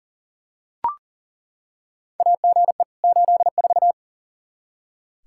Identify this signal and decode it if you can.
Morse code, heard as a keyed tone, about 40 wpm, 700 Hz. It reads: AGE 84
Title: Practice QSO - Exchange 3 - More Info - 40wpm